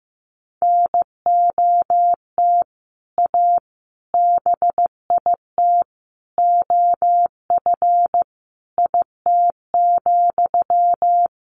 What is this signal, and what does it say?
Morse code, heard as a keyed tone, about 15 wpm, 700 Hz. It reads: NOT A BIT OF IT,
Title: Sentences from Top 1400 Words 15wpm